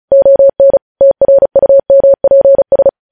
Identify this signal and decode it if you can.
Morse code, heard as a keyed tone, about 36 wpm, 550 Hz. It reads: ON TRUMPS
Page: news headlines in morse code at 35 wpm